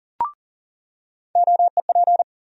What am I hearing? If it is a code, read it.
Morse code, heard as a keyed tone, about 40 wpm, 700 Hz. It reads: OEP
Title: USA Call Sign Postfixes - 3 Letters 40wpm